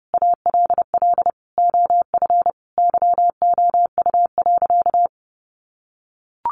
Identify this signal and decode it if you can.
Morse code, heard as a keyed tone, about 30 wpm, 700 Hz. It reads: ALL OF YOU.
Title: Sentences from Top 1400 Words 30wpm